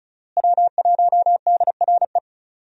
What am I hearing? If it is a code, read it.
Morse code, heard as a keyed tone, about 36 wpm, 700 Hz. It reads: W1DRE
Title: USA Call Signs - 5 Characters - Repeated 2x - 35wpm